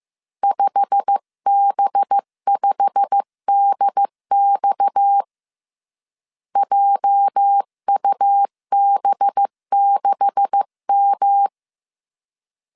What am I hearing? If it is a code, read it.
Morse code, heard as a keyed tone, about 15 wpm, 800 Hz. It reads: 5B5DX JUB6M
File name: CW / DET / 1 9 1500C S B 14wpm 3302